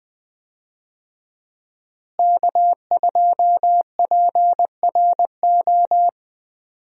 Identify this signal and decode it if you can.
Morse code, heard as a keyed tone, about 20 wpm, 700 Hz. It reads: K2PRO
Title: USA Call Signs - 5 Characters - Repeated 3x - 20wpm